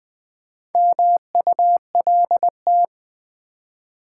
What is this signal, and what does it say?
Morse code, heard as a keyed tone, about 20 wpm, 700 Hz. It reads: MULT
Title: Build A Word - 2 Letters At A Time - 8-Letter Words - 20wpm